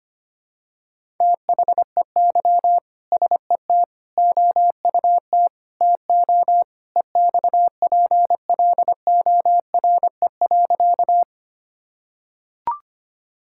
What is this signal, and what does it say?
Morse code, heard as a keyed tone, about 25 wpm, 700 Hz. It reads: THEY SET OUT TO EXPLORE.
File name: practice / 25wpm / Sentences with 2-Word Phrases - 25wpm